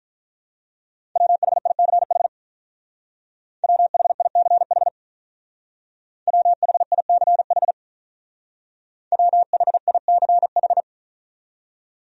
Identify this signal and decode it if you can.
Morse code, heard as a keyed tone, about 47 wpm, 700 Hz. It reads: WHICH WHICH WHICH WHICH